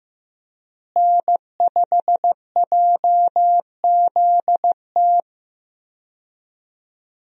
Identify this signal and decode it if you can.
Morse code, heard as a keyed tone, about 15 wpm, 700 Hz. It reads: N5JZT
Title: USA Call Signs - 5 Characters - Repeated 2x - 15wpm